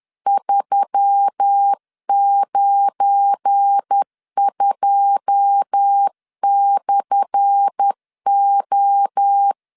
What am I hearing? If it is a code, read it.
Morse code, heard as a keyed tone, about 10 wpm, 800 Hz. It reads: 392/O
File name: LEZIONI / 10 SIGNS / 10A SLASH 10wpm 2808